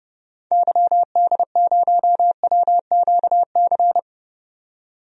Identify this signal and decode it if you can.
Morse code, heard as a keyed tone, about 30 wpm, 700 Hz. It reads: YD0WQC
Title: DXCC-100 Call Signs - 6 Characters - 30wpm